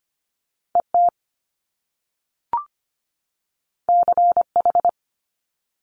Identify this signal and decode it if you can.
Morse code, heard as a keyed tone, about 25 wpm, 700 Hz. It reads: ET CH
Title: Top Two Letter Combinations in Words 25wpm